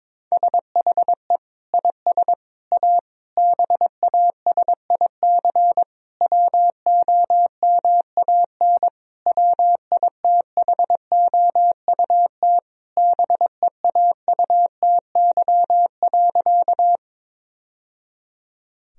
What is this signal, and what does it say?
Morse code, heard as a keyed tone, about 22 wpm, 700 Hz. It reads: SHE IS A BASIC WOMAN WITHOUT BEAUTY.
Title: Sentences from Top 1000 Words - Review Part 3 - 22wpm